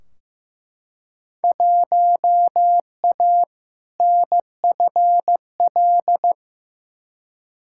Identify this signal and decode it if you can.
Morse code, heard as a keyed tone, about 15 wpm, 700 Hz. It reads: 1A NFL
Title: Field Day Exchanges Encore - 15wpm